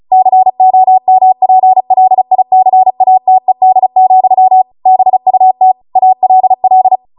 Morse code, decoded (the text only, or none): COMPLICATED, BUT ALL